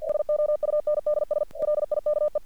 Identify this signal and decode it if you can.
Morse code, heard as a keyed tone, about 48 wpm, 600 Hz. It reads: SOUNDS LIKE
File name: archived / CRSnet